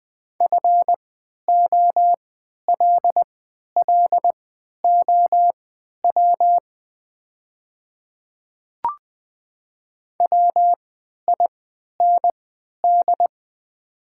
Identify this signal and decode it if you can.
Morse code, heard as a keyed tone, about 20 wpm, spaced as for 11 wpm, 700 Hz. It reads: FOLLOW WIND
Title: 3x Character Spacing - Top 500 Words - 20wpm